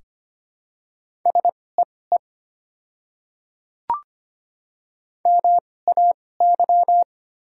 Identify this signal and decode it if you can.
Morse code, heard as a keyed tone, about 25 wpm, spaced as for 18 wpm, 690 Hz. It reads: SEE MAY
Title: 2x Character Spacing - Top 100 Words - 25wpm